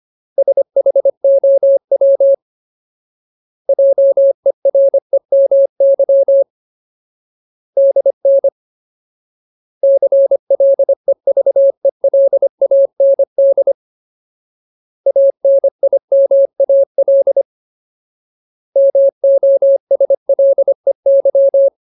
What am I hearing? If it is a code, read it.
Morse code, heard as a keyed tone, about 25 wpm, 550 Hz. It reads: SHOW JEREMY DN CLEVELAND ANIMAL MOSLEY
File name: Ultimate Word List / 25WPM Ultimate